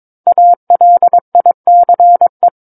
Morse code, heard as a keyed tone, about 22 wpm, 700 Hz. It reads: ALICE
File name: 06-Chapter-06 22 WPM